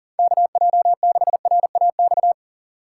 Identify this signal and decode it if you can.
Morse code, heard as a keyed tone, about 40 wpm, 700 Hz. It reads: KJ6RAX
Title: USA Call Signs - 6 Characters - 4x Character Spacing - 40wpm